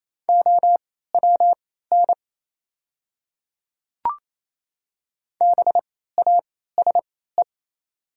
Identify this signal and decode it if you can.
Morse code, heard as a keyed tone, about 28 wpm, spaced as for 16 wpm, 700 Hz. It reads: OWN BASE
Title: 3x Character Spacing - Top 300 Words - 28wpm